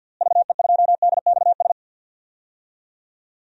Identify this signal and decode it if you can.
Morse code, heard as a keyed tone, about 46 wpm, 700 Hz. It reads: VE2DXS
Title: DXCC-100 Call Signs - 6 Characters - Repeated 3x - 50wpm